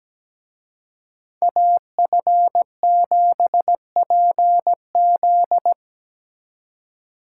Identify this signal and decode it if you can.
Morse code, heard as a keyed tone, about 20 wpm, 700 Hz. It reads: AF7PZ
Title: Speed Racing - USA 2x2 Call Signs 17wpm